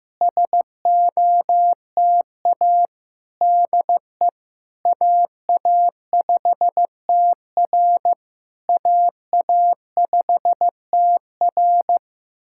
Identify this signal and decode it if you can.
Morse code, heard as a keyed tone, about 15 wpm, 700 Hz. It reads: SOTA DE AA5TR AA5TR
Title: SOTA QSO Practice 15wpm